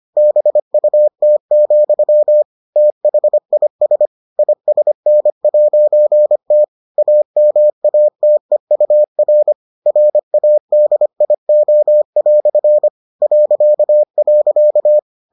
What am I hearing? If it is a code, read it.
Morse code, heard as a keyed tone, about 25 wpm, 600 Hz. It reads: BUT, THIS ISN'T AMATEUR RADIO" ..